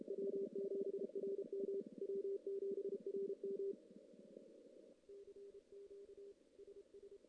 Morse code, heard as a keyed tone, about 31 wpm, 400 Hz. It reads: W7RM W7RM CQ SS
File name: CW / CW-SS-k5tr- -1700